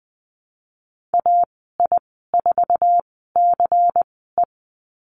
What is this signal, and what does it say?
Morse code, heard as a keyed tone, about 20 wpm, spaced as for 15 wpm, 700 Hz. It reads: AI4CE
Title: 122 Lesson 20g - Number 3 - Call Signs 20wpm